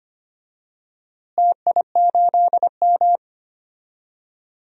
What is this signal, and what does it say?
Morse code, heard as a keyed tone, about 25 wpm, 700 Hz. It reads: TI8M
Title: DXCC-100 Call Signs - 4 Characters - Repeated 2x - 25wpm